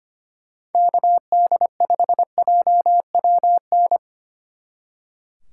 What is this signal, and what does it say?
Morse code, heard as a keyed tone, about 32 wpm, 700 Hz. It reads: KD5JWN